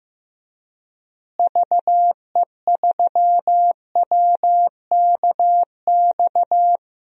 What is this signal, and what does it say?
Morse code, heard as a keyed tone, about 15 wpm, 700 Hz. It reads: VE3WKX
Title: DXCC-100 Call Signs - 6 Characters - Repeated 3x - 15wpm